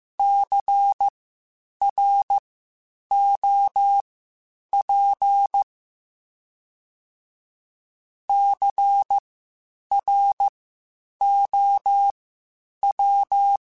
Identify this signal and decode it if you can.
Morse code, heard as a keyed tone, about 15 wpm, spaced as for 8 wpm, 800 Hz. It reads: CROP CROW
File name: Spaced Alphas / Spaced Alphas mp3s / Cs sp @13